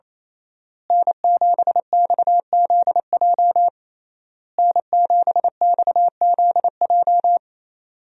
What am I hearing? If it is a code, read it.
Morse code, heard as a keyed tone, about 28 wpm, 700 Hz. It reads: N7XZJ N7XZJ
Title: USA Call Signs - 5 Characters - Repeated 3x - 28wpm